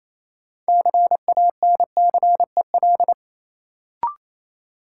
Morse code, heard as a keyed tone, about 28 wpm, 700 Hz. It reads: CANCEL
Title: Sentences from Top 1200 Words - New Word Drill - 28wpm